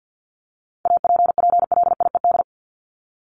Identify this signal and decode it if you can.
Morse code, heard as a keyed tone, about 52 wpm, 700 Hz. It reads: APPLIED
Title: ED Suffix - Top 100 Words with Base Words 50wpm